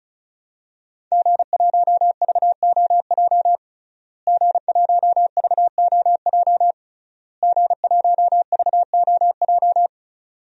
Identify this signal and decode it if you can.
Morse code, heard as a keyed tone, about 34 wpm, 700 Hz. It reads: G1VOJ G1VOJ G1VOJ
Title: DXCC-100 Call Signs - 5 Characters - Repeated 3x - 35wpm